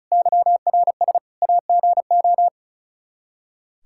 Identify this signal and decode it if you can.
Morse code, heard as a keyed tone, about 36 wpm, 700 Hz. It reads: YRS AGO
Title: K1USN SST QSO Practice 35wpm